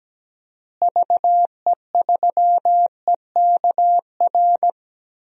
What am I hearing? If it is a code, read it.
Morse code, heard as a keyed tone, about 17 wpm, 700 Hz. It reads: VE3EKR
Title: DXCC-100 Call Signs - 6 Characters - Repeated 3x - 17wpm